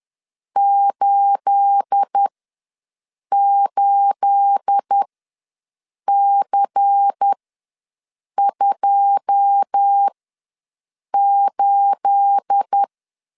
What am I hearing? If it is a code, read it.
Morse code, heard as a keyed tone, about 10 wpm, spaced as for 6 wpm, 800 Hz. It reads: 88C28
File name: LEZIONI / 9 28 ws 10wpm 4124